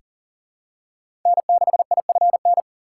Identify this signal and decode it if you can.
Morse code, heard as a keyed tone, about 37 wpm, 700 Hz. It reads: N6IFN
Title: USA Call Signs - 2x Character Spacing - 40wpm